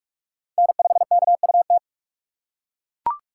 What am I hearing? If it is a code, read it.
Morse code, heard as a keyed tone, about 43 wpm, 700 Hz. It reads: N5KUT